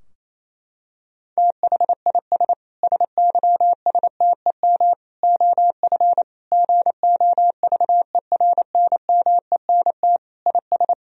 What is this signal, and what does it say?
Morse code, heard as a keyed tone, about 28 wpm, 700 Hz. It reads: THIS SYSTEM OF GOVERNMENT IS
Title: Sentences from Top 1000 Words - Review Part 3 - 28wpm